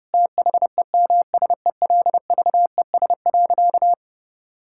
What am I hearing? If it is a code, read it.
Morse code, heard as a keyed tone, about 30 wpm, 700 Hz. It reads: THEMSELVES.
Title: Sentences from Top 1000 Words - Review Part 5 - 30wpm